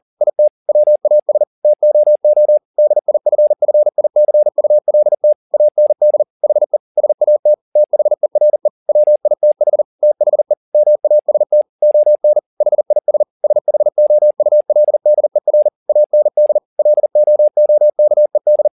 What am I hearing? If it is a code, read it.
Morse code, heard as a keyed tone, about 40 wpm, 600 Hz. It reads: IT WAS TOO DIFFICULT AND HE SAT THERE WITH THE MAST ON HIS SHOULDER AND LOOKED
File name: the old man and the sea 2 40wpm